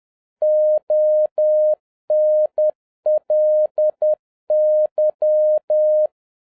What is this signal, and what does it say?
Morse code, heard as a keyed tone, about 10 wpm, 610 Hz. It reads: ONLY